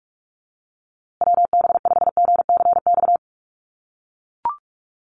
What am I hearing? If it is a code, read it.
Morse code, heard as a keyed tone, about 44 wpm, 700 Hz. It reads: WB5GCX
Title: USA Call Signs Repeated 2x 45wpm